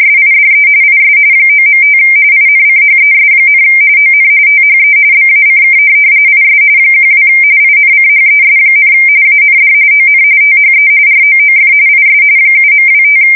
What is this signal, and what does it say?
RTTY, about 45.45 baud. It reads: SIDE IS 9.5 INCHES, ON THE BACK IS 12 INCHES, AND ON THE ROOF IS 7.25
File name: fsk170 / USS-New-Jersey